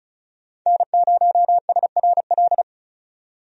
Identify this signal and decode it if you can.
Morse code, heard as a keyed tone, about 36 wpm, 700 Hz. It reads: N0SRL
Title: USA Call Signs - 5 Characters - 35wpm